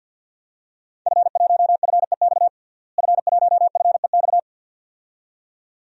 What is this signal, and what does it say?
Morse code, heard as a keyed tone, about 52 wpm, 700 Hz. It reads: F1FEX F1FEX
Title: DXCC-100 Call Signs - 5 Characters - Repeated 2x - 50wpm